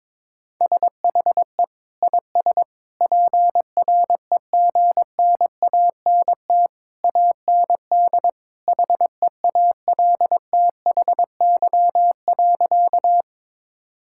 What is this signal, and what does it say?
Morse code, heard as a keyed tone, about 22 wpm, 700 Hz. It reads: SHE IS PREGNANT AND HEALTHY.
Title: Sentences from Top 1000 Words - Review Part 6 - 22wpm